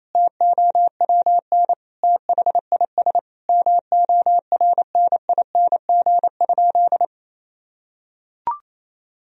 Morse code, heard as a keyed tone, about 28 wpm, 700 Hz. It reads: TOWN THIS MORNING?
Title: Sentences from Top 1000 Words - Review Part 2 - 28wpm